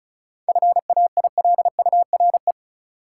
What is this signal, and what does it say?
Morse code, heard as a keyed tone, about 36 wpm, 700 Hz. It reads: FAILURE
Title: Sentences from Top 900 Words - New Word Drill - 35wpm